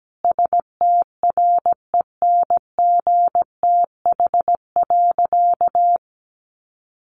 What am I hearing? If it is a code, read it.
Morse code, heard as a keyed tone, about 17 wpm, 700 Hz. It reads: STRENGTH.
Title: Sentences from Top 1300 Words 17wpm